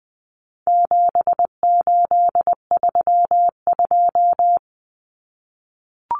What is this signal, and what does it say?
Morse code, heard as a keyed tone, about 20 wpm, 700 Hz. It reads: 7832
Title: Common Cut Numbers - 4-Digit 20wpm